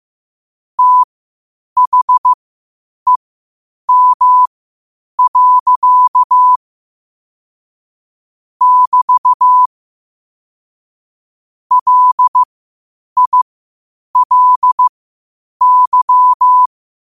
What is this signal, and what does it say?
Morse code, heard as a keyed tone, about 15 wpm, spaced as for 8 wpm, 1000 Hz. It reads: THEM. <BT> LILY
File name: qotdfiles / 8wpm